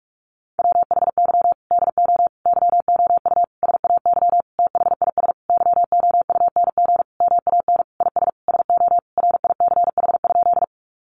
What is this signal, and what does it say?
Morse code, heard as a keyed tone, about 44 wpm, 700 Hz. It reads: WHY DO YOU SAY THIS YOUNG MAN IS SO RICH?
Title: Sentences from Top 1100 Words 45wpm